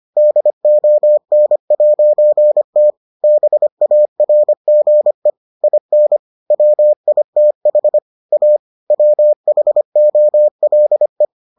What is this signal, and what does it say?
Morse code, heard as a keyed tone, about 25 wpm, 600 Hz. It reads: DON'T BARGE IN WITH A WHOLE